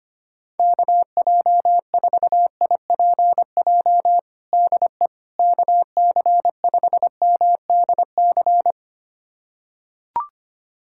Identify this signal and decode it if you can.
Morse code, heard as a keyed tone, about 25 wpm, 700 Hz. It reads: KJ4IPJ DE KC5MDC
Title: USA Callsign DE Callsign - 6 Characters - 25wpm